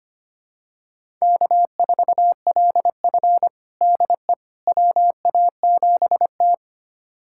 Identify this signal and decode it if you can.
Morse code, heard as a keyed tone, about 25 wpm, 700 Hz. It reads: K4LF DE WA7T
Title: USA Callsign DE Callsign - 4 Characters - 25wpm